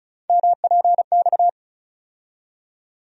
Morse code, heard as a keyed tone, about 36 wpm, 700 Hz. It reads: MPX
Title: USA Call Sign Postfixes - 3 Letters 35wpm